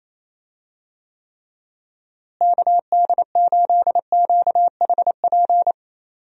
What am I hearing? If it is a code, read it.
Morse code, heard as a keyed tone, about 27 wpm, 700 Hz. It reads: KD8QHP
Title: USA Call Signs - 6 Characters - Repeated 2x - 28wpm